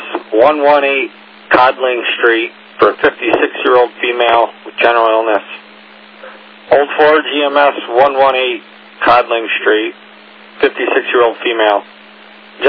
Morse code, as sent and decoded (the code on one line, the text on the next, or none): none